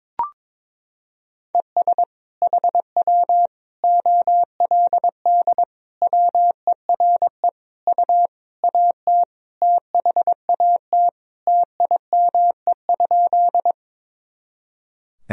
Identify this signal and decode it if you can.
Morse code, heard as a keyed tone, about 22 wpm, 700 Hz. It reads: ES HW OLD WERE U AT THAT TIME?
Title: Sentences from Top 200 Words and Abbreviations 22wpm